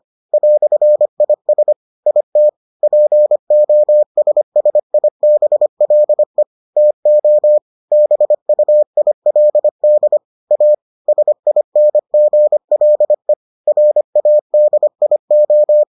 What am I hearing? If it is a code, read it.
Morse code, heard as a keyed tone, about 25 wpm, 600 Hz. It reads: "IS IT POSSIBLE TO BUILD A SINGLE RADIO